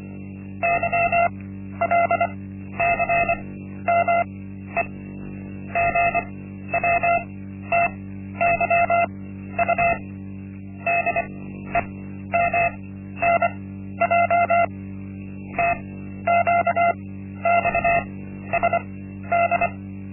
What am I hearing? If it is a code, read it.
Morse code, heard as a keyed tone, about 24 wpm, spaced as for 12 wpm, 690 Hz. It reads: YLCME GWTYU DEMNJ TQXSD
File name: Linksmakalnis-1